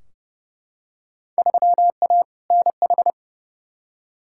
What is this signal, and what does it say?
Morse code, heard as a keyed tone, about 31 wpm, 700 Hz. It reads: 3A NH
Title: Field Day Exchanges Encore - 30wpm